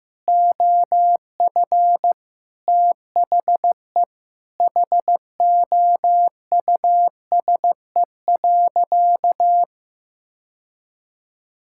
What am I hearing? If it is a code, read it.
Morse code, heard as a keyed tone, about 15 wpm, 700 Hz. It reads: OF THE HOUSE.
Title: Sentences from Top 1600 Words 15wpm